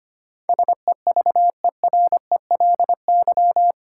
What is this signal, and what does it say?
Morse code, heard as a keyed tone, about 25 wpm, 700 Hz. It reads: SEVERELY
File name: Eight Letter Words - 25wpm